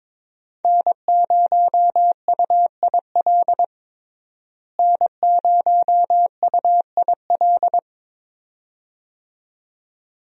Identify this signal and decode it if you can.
Morse code, heard as a keyed tone, about 22 wpm, 700 Hz. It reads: N0UIL N0UIL
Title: USA Call Signs - 5 Characters - Repeated 2x - 22wpm